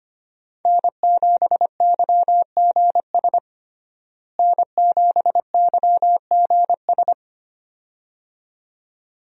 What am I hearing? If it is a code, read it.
Morse code, heard as a keyed tone, about 25 wpm, 700 Hz. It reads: N7YGS N7YGS
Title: USA Call Signs - 5 Characters - Repeated 2x - 25wpm